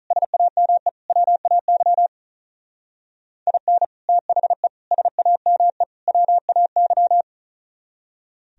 Morse code, heard as a keyed tone, about 43 wpm, 700 Hz. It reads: SAME WAY IN THE SAME WAY
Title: Speed-Racing - Sets of 4 Words 35wpm